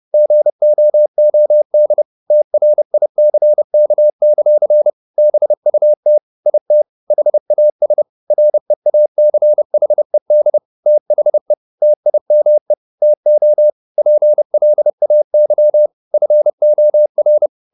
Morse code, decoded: GOOD TRICK; BUT IT HAS REACHED THE TIME TO PLAY FOR

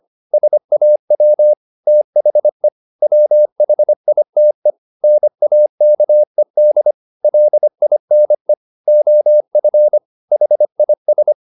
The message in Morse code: SAW THE WHITE NAKED LINE OF HIS

